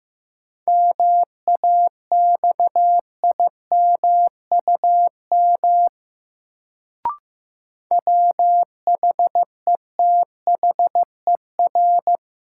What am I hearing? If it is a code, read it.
Morse code, heard as a keyed tone, about 15 wpm, 700 Hz. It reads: MAXIMUM WHETHER
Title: Sentences from Top 1300 Words - New Word Drill Alt - 15wpm